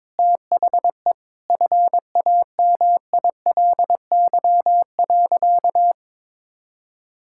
Morse code, heard as a keyed tone, about 22 wpm, 700 Hz. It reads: THE FAMILY.
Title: Sentences from Top 1000 Words - Review Part 5 - 22wpm